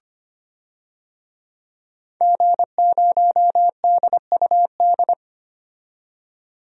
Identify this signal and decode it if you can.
Morse code, heard as a keyed tone, about 25 wpm, 700 Hz. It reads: G0DUD
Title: DXCC-100 Call Signs - 5 Characters - 25wpm